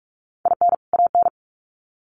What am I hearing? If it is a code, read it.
Morse code, heard as a keyed tone, about 44 wpm, 700 Hz. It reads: IN AN
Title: Sets of 2 Words 45wpm